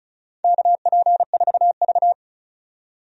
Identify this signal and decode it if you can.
Morse code, heard as a keyed tone, about 36 wpm, 690 Hz. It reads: KP4V
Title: DXCC-100 Call Signs - 4 Characters - 35wpm